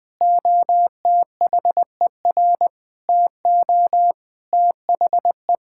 Morse code, heard as a keyed tone, about 20 wpm, 700 Hz. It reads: OTHER TO THE
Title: Sentences from Top 1000 Words - Review Part 6 - 20wpm